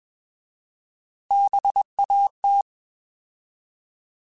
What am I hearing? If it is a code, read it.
Morse code, heard as a keyed tone, about 21 wpm, 800 Hz. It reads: BAT
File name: Bs @20